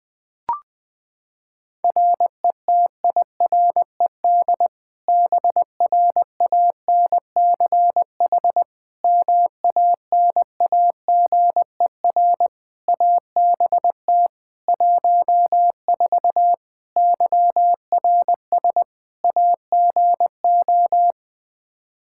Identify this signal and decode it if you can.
Morse code, heard as a keyed tone, about 20 wpm, 700 Hz. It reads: RETIRED BRANCH MANAGER ABT 14 YRS AGO
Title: Practice QSO - Exchange 3 - More Info - 20wpm